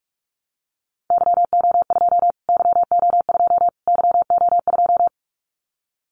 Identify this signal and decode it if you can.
Morse code, heard as a keyed tone, about 44 wpm, 700 Hz. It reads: YO2 YO2 YO2
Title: Callsign Prefixes for Top 25 DXCC - Repeated 3x - 45wpm